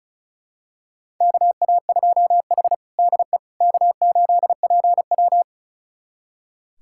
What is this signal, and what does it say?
Morse code, heard as a keyed tone, about 34 wpm, 700 Hz. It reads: KA2H DE K8PW
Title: USA Callsign DE Callsign - 4 Characters - 35wpm